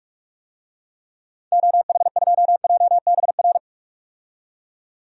Morse code, heard as a keyed tone, about 46 wpm, 700 Hz. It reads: OH2JBR
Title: DXCC-100 Call Signs - 6 Characters - 45wpm